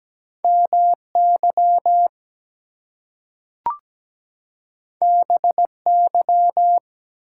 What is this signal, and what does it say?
Morse code, heard as a keyed tone, about 17 wpm, 700 Hz. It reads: MY BY
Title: Two Letter Words - 17wpm